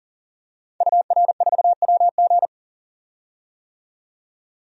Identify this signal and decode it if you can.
Morse code, heard as a keyed tone, about 41 wpm, 700 Hz. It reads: UR4WG
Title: DXCC-100 Call Signs - 5 Characters - 40wpm